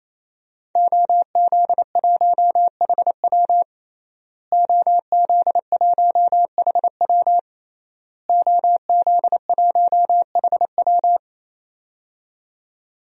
Morse code, heard as a keyed tone, about 28 wpm, 700 Hz. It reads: OZ1HW OZ1HW OZ1HW
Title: DXCC-100 Call Signs - 5 Characters - Repeated 3x - 28wpm